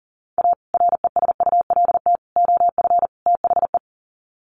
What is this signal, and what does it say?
Morse code, heard as a keyed tone, about 40 wpm, 700 Hz. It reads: A RESULT OF THE
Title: Sets of 4 Words 40wpm